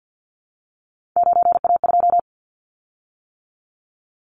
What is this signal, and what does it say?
Morse code, heard as a keyed tone, about 50 wpm, 700 Hz. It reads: 9A2
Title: Callsign Prefixes for Top 50 DXCC - 50wpm